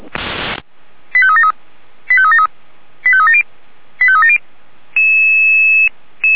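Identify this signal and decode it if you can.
Five-tone selective call (ZVEI), heard as quick runs of five tones, tones of about 70 ms. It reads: 85373 85373 85370 85370
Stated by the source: Five-tone ZVEI fire brigade selective call received on an SDR, from the feuerwehr-orlamuende audio server.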